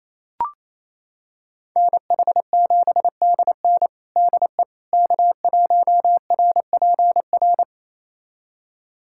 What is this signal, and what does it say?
Morse code, heard as a keyed tone, about 28 wpm, 700 Hz. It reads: NH7DN DE K1RPR
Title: USA Callsign DE Callsign 28wpm